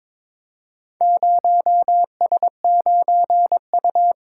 Morse code, heard as a keyed tone, about 22 wpm, 700 Hz. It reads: 0S9U
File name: practice / 22wpm / Four Letter-Number Combinations - 22wpm